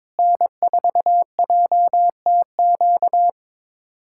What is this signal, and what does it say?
Morse code, heard as a keyed tone, about 22 wpm, 700 Hz. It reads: N4JTQ
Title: USA Callsign DE Callsign - 5 Characters - 22wpm